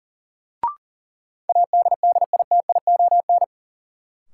Morse code, heard as a keyed tone, about 40 wpm, 700 Hz. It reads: ADDITION
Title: Sets of 3 Words - New Word Drill - 40wpm